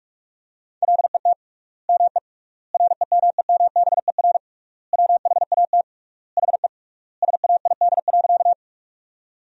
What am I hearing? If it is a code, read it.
Morse code, heard as a keyed tone, about 46 wpm, 700 Hz. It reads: LET ME REMEMBER WHAT HE SAID.
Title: 3x Word-Spacing - Sentences from Top 400 Words - 45wpm